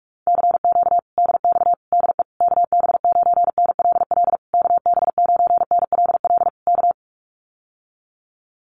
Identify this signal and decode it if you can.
Morse code, heard as a keyed tone, about 44 wpm, 700 Hz. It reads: CQ DX DE KB9NLL KB9NLL K
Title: Common QSO Phrases Encore 45wpm